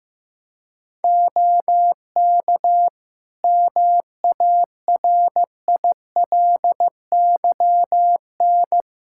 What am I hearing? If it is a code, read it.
Morse code, heard as a keyed tone, about 16 wpm, 700 Hz. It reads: OK MARILYN
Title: Practice QSO - Exchange 2 - Info - 15wpm